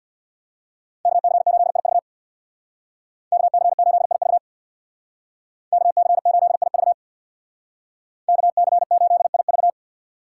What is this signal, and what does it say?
Morse code, heard as a keyed tone, about 61 wpm, 700 Hz. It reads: KC8IV KC8IV KC8IV KC8IV
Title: Speed Racing - USA 2x2 Call Signs 50wpm